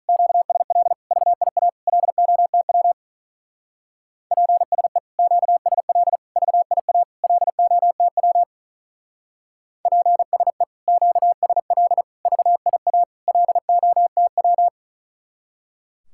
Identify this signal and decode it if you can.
Morse code, heard as a keyed tone, about 44 wpm, 700 Hz. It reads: QSL VIA LOTW PSE QSL VIA LOTW PSE QSL VIA LOTW